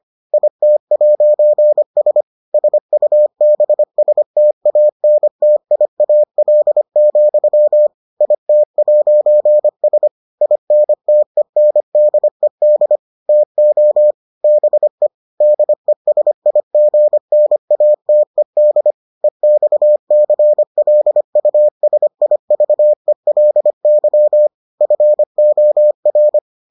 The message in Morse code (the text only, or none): IT'S SUBSTANTIAL, IT'S INTENDED TO BE DESIGNATED EXCLUSIVELY FOR